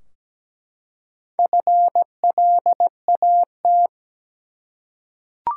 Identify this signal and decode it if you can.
Morse code, heard as a keyed tone, about 17 wpm, 700 Hz. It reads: FLAT
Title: Four Letter Words - 17wpm